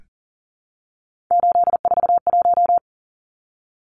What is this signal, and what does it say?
Morse code, heard as a keyed tone, about 40 wpm, 700 Hz. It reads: 841
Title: Numbers - 3-Digit 40wpm